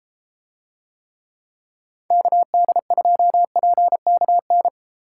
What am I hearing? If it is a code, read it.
Morse code, heard as a keyed tone, about 32 wpm, 700 Hz. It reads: KD2PKN